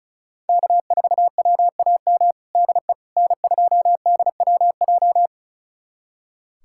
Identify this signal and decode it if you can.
Morse code, heard as a keyed tone, about 36 wpm, 700 Hz. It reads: K4WAM DE N2DWJ
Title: USA Callsign DE Callsign - 5 Characters - 35wpm